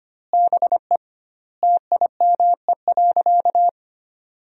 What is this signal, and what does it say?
Morse code, heard as a keyed tone, about 25 wpm, 700 Hz. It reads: BE TIME.
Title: 2x Word-Spacing - Sentences from Top 200 Words - 25wpm